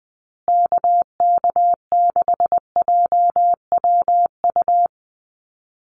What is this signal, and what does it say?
Morse code, heard as a keyed tone, about 20 wpm, 700 Hz. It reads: KK6JWU
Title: USA Call Signs Repeated 3x 20wpm